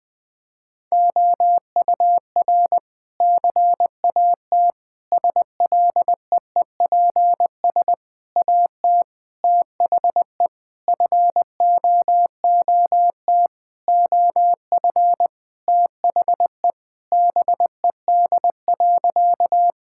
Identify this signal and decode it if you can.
Morse code, heard as a keyed tone, about 20 wpm, 700 Hz. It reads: OUR CAT SLEEPS AT THE FOOT OF THE BED.
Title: Sentences with 5-Word Phrases - 20wpm